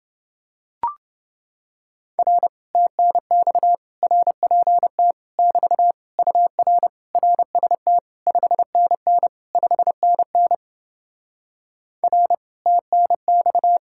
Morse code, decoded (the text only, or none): R TNX RPT <BT> UR RST 5NN 5NN R TNX